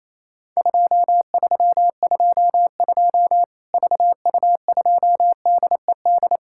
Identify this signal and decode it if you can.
Morse code, heard as a keyed tone, about 27 wpm, 700 Hz. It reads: 2322 VU2DED
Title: Worked All Europe DX Contest - QTC Exchange - 28wpm